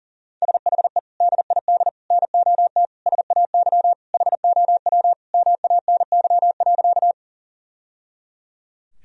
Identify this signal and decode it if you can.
Morse code, heard as a keyed tone, about 40 wpm, 700 Hz. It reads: SHE DID NOT SAY HOW MANY.
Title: Sentences from Top 1000 Words - Review Part 1 - 40wpm